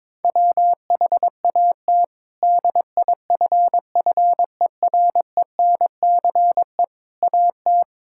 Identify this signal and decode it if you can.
Morse code, heard as a keyed tone, about 22 wpm, 700 Hz. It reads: WHAT DIFFERENCE AT
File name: practice / 22wpm / Sentences with 8-Word Phrases - 22wpm